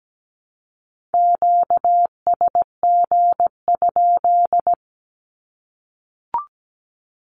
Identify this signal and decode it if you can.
Morse code, heard as a keyed tone, about 17 wpm, 700 Hz. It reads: QSG?
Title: Q-Codes 17wpm